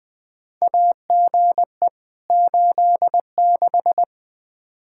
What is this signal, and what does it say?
Morse code, heard as a keyed tone, about 20 wpm, 700 Hz. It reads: AGE 86
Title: K1USN SST QSO Practice 20wpm